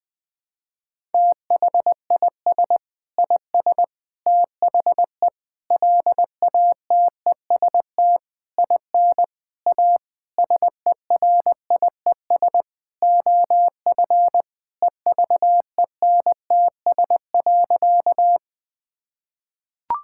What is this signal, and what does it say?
Morse code, heard as a keyed tone, about 20 wpm, 700 Hz. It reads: THIS IS THE LATEST IN A SERIES OF EVENTS.
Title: Sentences with 6-Word Phrases - Spoken Beginning - 20wpm